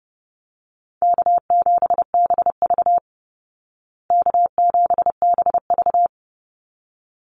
Morse code, heard as a keyed tone, about 30 wpm, 700 Hz. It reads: K7BV K7BV
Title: Easy Call Signs Repeated 2x - 30wpm